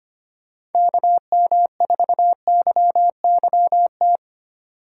Morse code, heard as a keyed tone, about 30 wpm, 700 Hz. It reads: KM4YYT
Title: Speed Racing - USA 2x3 Call Signs 25wpm